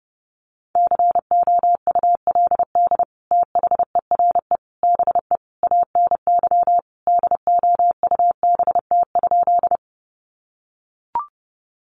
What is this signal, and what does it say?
Morse code, heard as a keyed tone, about 30 wpm, 700 Hz. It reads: COULD THERE BE ANY DOUBT?
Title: Sentences from Top 1500 Words 30wpm